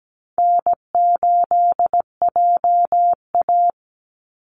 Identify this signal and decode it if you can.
Morse code, heard as a keyed tone, about 17 wpm, 700 Hz. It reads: N8JA
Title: Easy Call Signs Repeated 2x - 17wpm